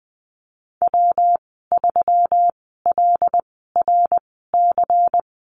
Morse code, heard as a keyed tone, about 20 wpm, spaced as for 15 wpm, 700 Hz. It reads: W3LRC
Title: 147 Lesson 24g - Letter y - Call Signs 20wpm